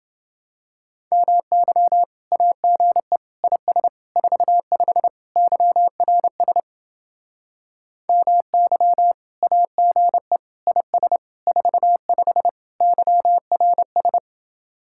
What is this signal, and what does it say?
Morse code, heard as a keyed tone, about 29 wpm, 700 Hz. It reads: MY AGE IS 45 YRS MY AGE IS 45 YRS